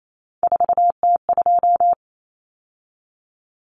Morse code, heard as a keyed tone, about 28 wpm, 700 Hz. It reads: 4T2
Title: Common Cut Numbers - 3-Digit 28wpm